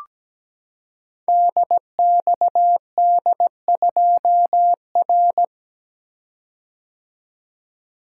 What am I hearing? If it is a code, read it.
Morse code, heard as a keyed tone, about 17 wpm, 700 Hz. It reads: DXD2R
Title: Code Groups - 5 Letter:Numbers - 17wpm